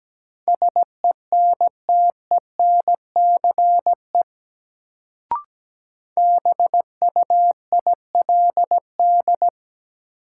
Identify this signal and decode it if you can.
Morse code, heard as a keyed tone, about 17 wpm, 700 Hz. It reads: SENTENCE BUILD
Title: Sentences from Top 1000 Words - Review Part 1 - New Word Drill - 17wpm